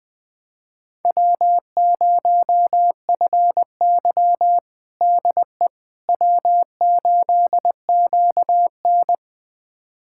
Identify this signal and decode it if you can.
Morse code, heard as a keyed tone, about 20 wpm, 700 Hz. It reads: W0FY DE W8QN
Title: USA Callsign DE Callsign - 4 Characters - 20wpm